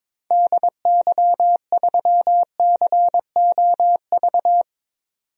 Morse code, heard as a keyed tone, about 21 wpm, 700 Hz. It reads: DY3COV
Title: DXCC-100 Call Signs - 6 Characters - 22wpm